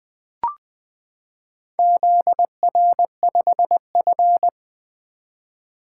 Morse code, heard as a keyed tone, about 20 wpm, 700 Hz. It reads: ZR5F
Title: Four Letter-Number Combinations - 20wpm